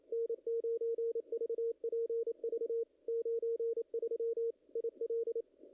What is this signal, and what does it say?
Morse code, heard as a keyed tone, about 25 wpm, 440 Hz, with radio noise. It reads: N9VPV 93 IL
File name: contests / ss-cw-k5tr-op-n5rz -0000